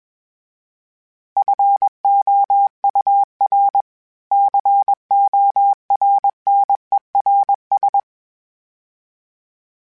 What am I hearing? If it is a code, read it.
Morse code, heard as a keyed tone, about 21 wpm, 800 Hz. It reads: FOUR CORNERS
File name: fromSMs / FromSMs mp3s / Cities@20